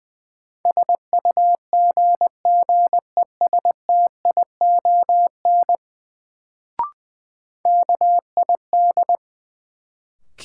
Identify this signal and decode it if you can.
Morse code, heard as a keyed tone, about 21 wpm, 700 Hz. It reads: SUGGESTION KID
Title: Sentences from Top 1600 Words - New Word Drill - 20wpm